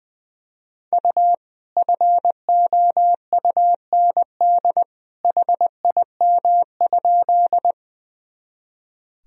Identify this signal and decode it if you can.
Morse code, heard as a keyed tone, about 20 wpm, 700 Hz. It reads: U FOUND HIM?
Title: Sentences from Top 300 Words and Abbreviations 20wpm